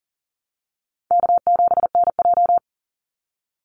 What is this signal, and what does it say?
Morse code, heard as a keyed tone, about 40 wpm, 700 Hz. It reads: K7NJ
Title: Call Signs Repeated 3x - 40wpm